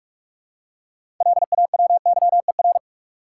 Morse code, heard as a keyed tone, about 46 wpm, 700 Hz. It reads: LAWYER
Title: Six Letter Words - 45wpm